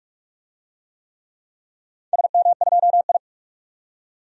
Morse code, heard as a keyed tone, about 44 wpm, 700 Hz. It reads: SM2I
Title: DXCC-100 Call Signs - 4 Characters - 45wpm